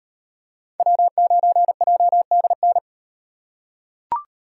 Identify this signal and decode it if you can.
Morse code, heard as a keyed tone, about 37 wpm, 700 Hz. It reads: W9JDN